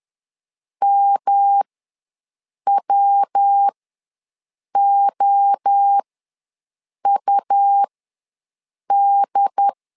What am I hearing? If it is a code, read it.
Morse code, heard as a keyed tone, about 10 wpm, spaced as for 6 wpm, 800 Hz. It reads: MWOUD
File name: LEZIONI / 3 MWRO / 3 MWRO ws 10wpm 2310